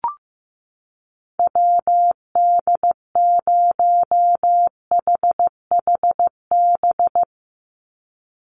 Morse code, heard as a keyed tone, about 15 wpm, 700 Hz. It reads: WD0HHB
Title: USA Call Signs 15wpm